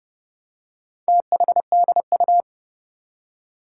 Morse code, heard as a keyed tone, about 30 wpm, 690 Hz. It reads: THDU
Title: Four Challenging Letter-Number Combinations 30wpm